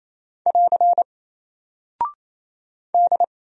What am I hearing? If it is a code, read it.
Morse code, heard as a keyed tone, about 27 wpm, 700 Hz. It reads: <AR> D
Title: Single Character (Letter:Number:Punctuation:Prosign) Alt 28wpm